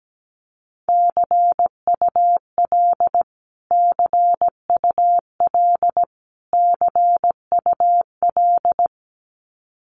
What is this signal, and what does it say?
Morse code, heard as a keyed tone, about 17 wpm, 700 Hz. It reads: CUL CUL CUL
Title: Abbreviations Repeated 3x - 17wpm